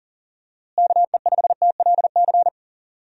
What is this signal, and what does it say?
Morse code, heard as a keyed tone, about 40 wpm, 700 Hz. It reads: KE5TLC
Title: USA Call Signs - 6 Characters - Repeated 3x - 40wpm